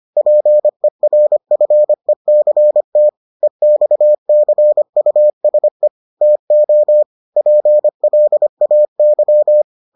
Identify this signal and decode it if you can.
Morse code, heard as a keyed tone, about 25 wpm, 600 Hz. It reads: PERFECT EXCUSE TO PLAY